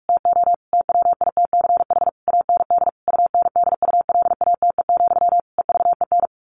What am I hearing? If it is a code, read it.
Morse code, heard as a keyed tone, about 46 wpm, 700 Hz. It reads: TO TWITCH AND UNDULATE, EVEN
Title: A Sound Of Thunder 45wpm